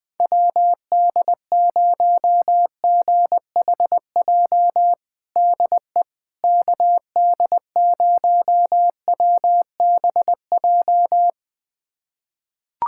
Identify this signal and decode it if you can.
Morse code, heard as a keyed tone, about 21 wpm, 700 Hz. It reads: WD0GHJ DE KD0WBJ
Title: USA Callsign DE Callsign - 6 Characters - 20wpm